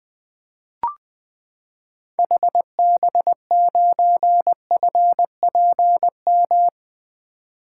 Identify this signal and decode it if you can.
Morse code, heard as a keyed tone, about 20 wpm, 700 Hz. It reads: HB9FPM
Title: 2023 SOTA Activator Call Signs - 20wpm